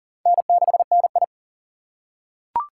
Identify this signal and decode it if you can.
Morse code, heard as a keyed tone, about 40 wpm, 700 Hz. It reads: N6NI
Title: USA Call Signs - 4 Characters - 4x Character Spacing - 40wpm